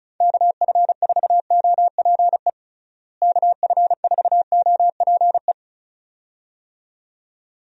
Morse code, heard as a keyed tone, about 36 wpm, 700 Hz. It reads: KF4OPE KF4OPE
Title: USA Call Signs - 6 Characters - Repeated 2x - 35wpm